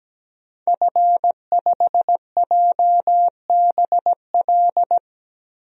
Morse code, heard as a keyed tone, about 17 wpm, 700 Hz. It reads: F5JBL
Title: DXCC-100 Call Signs - 5 Characters - 17wpm